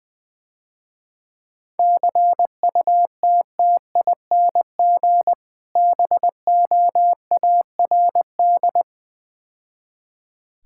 Morse code, heard as a keyed tone, about 20 wpm, 700 Hz. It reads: CUTTING BOARD
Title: Difficult Words - Speed-Racing with Hints - 20wpm